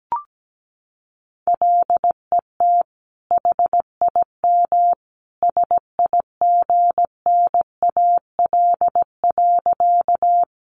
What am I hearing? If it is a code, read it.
Morse code, heard as a keyed tone, about 17 wpm, 700 Hz. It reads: LET HIM SIGNAL.
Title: Sentences from Top 1400 Words 17wpm